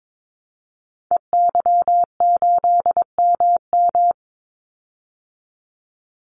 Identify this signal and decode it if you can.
Morse code, heard as a keyed tone, about 22 wpm, 700 Hz. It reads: EY8MM
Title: Call Signs Repeated 3x - 22wpm